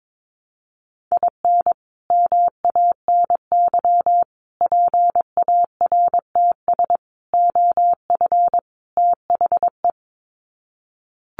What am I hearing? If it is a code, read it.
Morse code, heard as a keyed tone, about 22 wpm, 700 Hz. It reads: IN MANY PARTS OF THE
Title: Sets of 5 Words Encore 22wpm